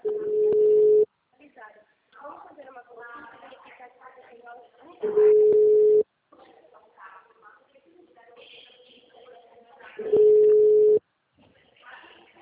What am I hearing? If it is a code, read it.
Ringback tone (Europe, 425 Hz), with speech.